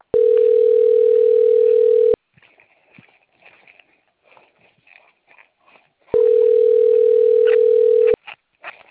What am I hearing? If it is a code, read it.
Ringback tone (North America), 440 + 480 Hz.